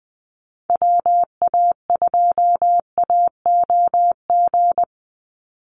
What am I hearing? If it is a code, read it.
Morse code, heard as a keyed tone, about 20 wpm, 700 Hz. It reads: WA2AOG
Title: Call Signs - 20wpm